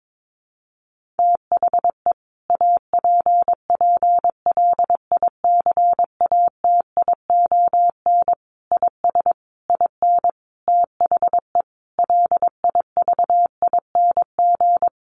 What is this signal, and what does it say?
Morse code, heard as a keyed tone, about 22 wpm, 700 Hz. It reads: THE APPLICATION IS IN THE LIVING
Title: Sentences from Top 1200 Words 22wpm